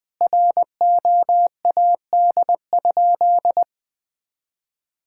Morse code, heard as a keyed tone, about 20 wpm, 700 Hz. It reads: ROAD?
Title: Sentences from Top 1000 Words - Review Part 5 - 20wpm